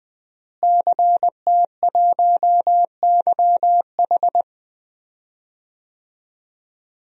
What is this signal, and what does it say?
Morse code, heard as a keyed tone, about 20 wpm, 700 Hz. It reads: CT1YH
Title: DXCC-100 Call Signs - 5 Characters - Repeated 2x - 20wpm